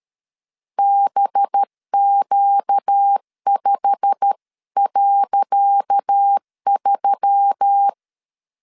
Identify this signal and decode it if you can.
Morse code, heard as a keyed tone, about 13 wpm, 800 Hz. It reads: BQ5.3
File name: LEZIONI / 10 SIGNS / 10D COMA 12wpm 2544